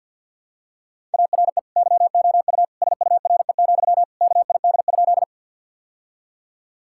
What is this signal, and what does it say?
Morse code, heard as a keyed tone, about 50 wpm, 700 Hz. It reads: ARE YOU SURE, KID?
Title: Sentences from Top 1600 Words 50wpm